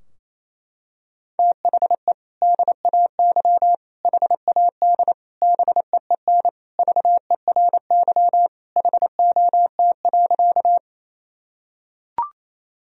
Morse code, heard as a keyed tone, about 28 wpm, 700 Hz. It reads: THE DAY HAD BEEN VERY HOT.
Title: Sentences from Top 1000 Words - Review Part 3 - 28wpm